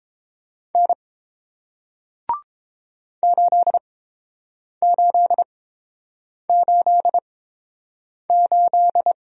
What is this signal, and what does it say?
Morse code, heard as a keyed tone, about 30 wpm, spaced as for 7 wpm, 700 Hz. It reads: N 8888